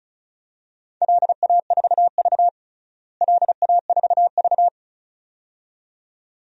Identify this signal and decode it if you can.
Morse code, heard as a keyed tone, about 36 wpm, 700 Hz. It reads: LA4V LA4V
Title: DXCC-100 Call Signs - 4 Characters - Repeated 2x - 35wpm